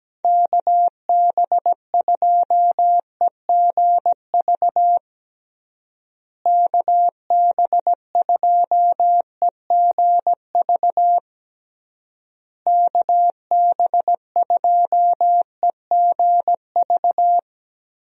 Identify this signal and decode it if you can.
Morse code, heard as a keyed tone, about 17 wpm, 700 Hz. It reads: KB2EGV KB2EGV KB2EGV
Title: USA Call Signs - 6 Characters - Repeated 3x - 17wpm